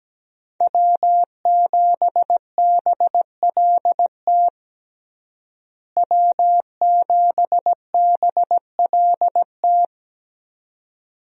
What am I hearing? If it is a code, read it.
Morse code, heard as a keyed tone, about 17 wpm, 700 Hz. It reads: W7BLT W7BLT
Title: USA Call Signs - 5 Characters - Repeated 2x - 17wpm